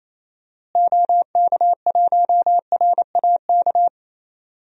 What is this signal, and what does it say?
Morse code, heard as a keyed tone, about 27 wpm, 700 Hz. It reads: OK1RAK
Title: DXCC-100 Call Signs - 6 Characters - Repeated 3x - 28wpm